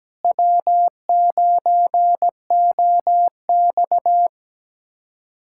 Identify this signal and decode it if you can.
Morse code, heard as a keyed tone, about 17 wpm, 700 Hz. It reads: W9OX
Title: USA Callsign DE Callsign - 4 Characters - 17wpm